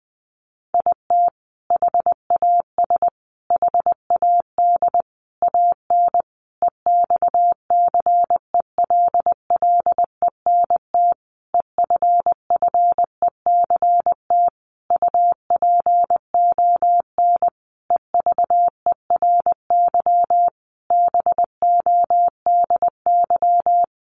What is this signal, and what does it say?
Morse code, heard as a keyed tone, about 19 wpm, 700 Hz. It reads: IT HAS HAD AN EXCELLENT EFFECT UPON EVERYBODY
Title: Sentences from Top 1100 Words 20wpm